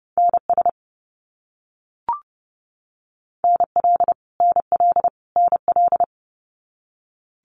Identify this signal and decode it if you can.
Morse code, heard as a keyed tone, about 30 wpm, 700 Hz. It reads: NS NL NL NL
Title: Canadian Provinces and Territories - Repeated 3x - 30wpm